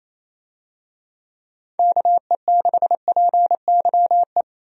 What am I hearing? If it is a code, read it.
Morse code, heard as a keyed tone, about 28 wpm, 700 Hz. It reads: KE6PYE
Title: USA Call Signs - 6 Characters - 28wpm